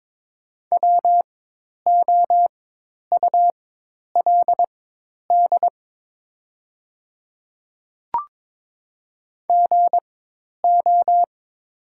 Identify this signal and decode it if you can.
Morse code, heard as a keyed tone, about 22 wpm, spaced as for 10 wpm, 700 Hz. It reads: WOULD GO
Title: 4x Character Spacing - Top 100 Words - 22wpm